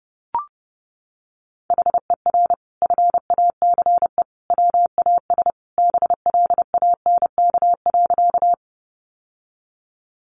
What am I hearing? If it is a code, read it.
Morse code, heard as a keyed tone, about 30 wpm, 700 Hz. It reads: HER FACE WAS BLANK.
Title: Sentences from Top 1500 Words 30wpm